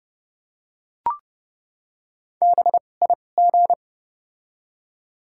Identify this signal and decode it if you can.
Morse code, heard as a keyed tone, about 31 wpm, spaced as for 22 wpm, 700 Hz. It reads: BIG
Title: 2x Character Spacing - Top 200 Words - 30wpm